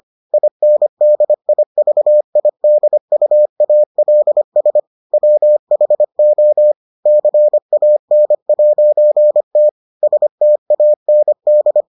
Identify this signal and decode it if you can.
Morse code, heard as a keyed tone, about 25 wpm, 600 Hz. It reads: INDIVIDUALS WHO CAN'T STAND